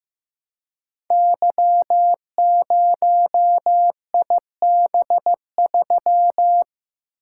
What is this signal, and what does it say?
Morse code, heard as a keyed tone, about 15 wpm, 700 Hz. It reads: Y0IB3
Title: Code Groups - 5 Letter:Numbers - 15wpm